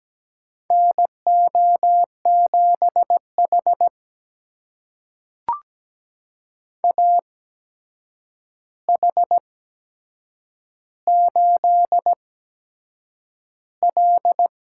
Morse code, heard as a keyed tone, about 17 wpm, spaced as for 7 wpm, 700 Hz. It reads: NO7H AH8L
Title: USA Call Signs - 4 Characters - 8x Character Spacing - 17wpm